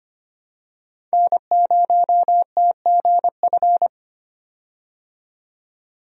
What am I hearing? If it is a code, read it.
Morse code, heard as a keyed tone, about 25 wpm, 700 Hz. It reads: N0TGF
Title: USA Call Signs - 5 Characters - 25wpm